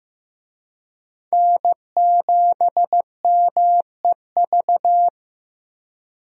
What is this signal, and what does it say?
Morse code, heard as a keyed tone, about 15 wpm, 700 Hz. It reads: N7MEV
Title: USA Call Signs - 5 Characters - Repeated 2x - 15wpm